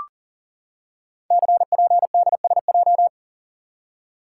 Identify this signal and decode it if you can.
Morse code, heard as a keyed tone, about 38 wpm, 700 Hz. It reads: CPDSJ
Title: Code Groups - 5 Letter:Numbers - 40wpm